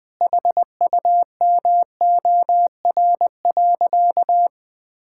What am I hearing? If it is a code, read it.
Morse code, heard as a keyed tone, about 20 wpm, 700 Hz. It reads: HUMOR.
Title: Sentences with 2-Word Phrases - Spoken Beginning - 20wpm